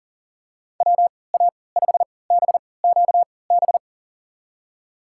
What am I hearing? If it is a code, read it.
Morse code, heard as a keyed tone, about 40 wpm, spaced as for 23 wpm, 700 Hz. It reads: WA5BQB
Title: USA Call Signs - 6 Characters - 3x Character Spacing - 40wpm